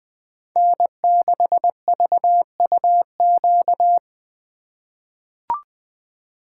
Morse code, heard as a keyed tone, about 20 wpm, 700 Hz. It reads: N6VUQ
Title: USA Call Signs - 5 Characters - 2x Character Spacing - 20wpm